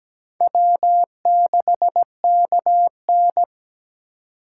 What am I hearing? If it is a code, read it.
Morse code, heard as a keyed tone, about 17 wpm, 700 Hz. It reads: W6KN
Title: USA Call Signs - 4 Characters - 8x Character Spacing - 17wpm